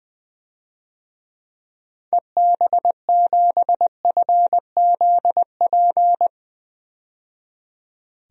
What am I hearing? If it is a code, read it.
Morse code, heard as a keyed tone, about 20 wpm, 700 Hz. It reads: EB7FZP
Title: DXCC-100 Call Signs - 6 Characters - 20wpm